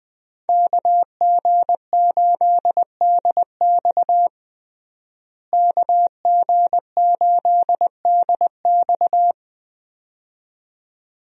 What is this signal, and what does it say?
Morse code, heard as a keyed tone, about 20 wpm, 700 Hz. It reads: KG8DX KG8DX
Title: USA Call Signs - 5 Characters - Repeated 2x - 20wpm